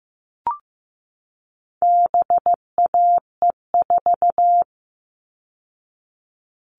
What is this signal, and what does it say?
Morse code, heard as a keyed tone, about 15 wpm, 700 Hz. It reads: BAE4
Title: All Cut Numbers - 4-Digit 15wpm